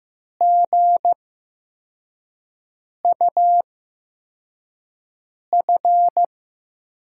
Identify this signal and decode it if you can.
Morse code, heard as a keyed tone, about 15 wpm, spaced as for 4 wpm, 700 Hz. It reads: GUF
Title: USA Call Signs - 8x Character Spacing - 15wpm